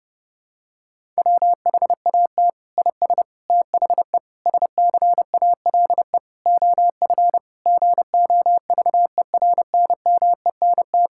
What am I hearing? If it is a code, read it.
Morse code, heard as a keyed tone, about 30 wpm, 700 Hz. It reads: WHAT IS THE SCALE OF GOVERNMENT
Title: Sentences from Top 1000 Words - Review Part 5 - 30wpm